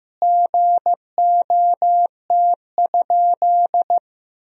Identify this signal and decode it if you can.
Morse code, heard as a keyed tone, about 15 wpm, 700 Hz. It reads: GOT?
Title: Sentences from Top 300 Words and Abbreviations 15wpm